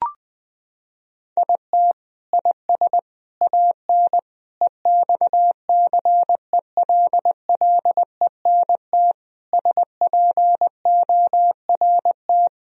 Morse code, heard as a keyed tone, about 20 wpm, 700 Hz. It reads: IT IS AN EXCELLENT SPORT
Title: Sentences from Top 1000 Words - Review Part 4 - 20wpm